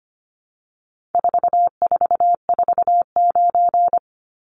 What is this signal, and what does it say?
Morse code, heard as a keyed tone, about 25 wpm, 700 Hz. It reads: 4449
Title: Numbers - 4-Digit 25wpm